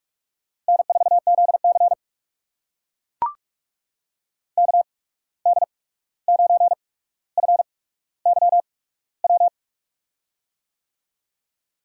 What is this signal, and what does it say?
Morse code, heard as a keyed tone, about 44 wpm, spaced as for 17 wpm, 700 Hz. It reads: N4ZC KD9FYW
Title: USA Call Signs - 8x Character Spacing - 45wpm